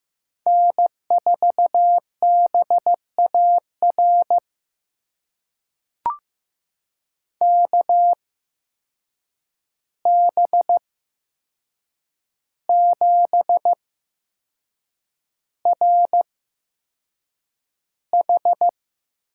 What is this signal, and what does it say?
Morse code, heard as a keyed tone, about 15 wpm, spaced as for 6 wpm, 700 Hz. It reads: N4BAR KB7RH
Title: USA Call Signs - 8x Character Spacing - 15wpm